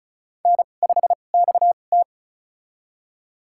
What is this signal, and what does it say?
Morse code, heard as a keyed tone, about 34 wpm, spaced as for 25 wpm, 700 Hz. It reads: N5XT
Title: USA Call Signs - 2x Character Spacing - 35wpm